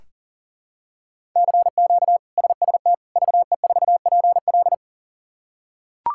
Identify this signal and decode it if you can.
Morse code, heard as a keyed tone, about 40 wpm, 700 Hz. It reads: CQ SST VE4PL
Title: K1USN SST QSO Practice - 40wpm